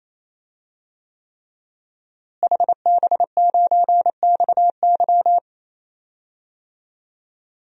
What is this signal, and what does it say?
Morse code, heard as a keyed tone, about 28 wpm, 700 Hz. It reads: HB9XY
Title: DXCC-100 Call Signs - 5 Characters - Repeated 3x - 28wpm